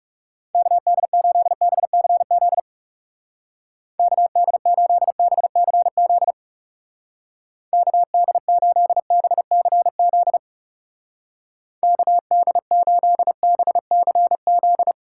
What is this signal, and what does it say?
Morse code, heard as a keyed tone, about 37 wpm, 700 Hz. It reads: KD8BCZ KD8BCZ KD8BCZ KD8BCZ
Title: Speed Racing - USA Call Signs 30wpm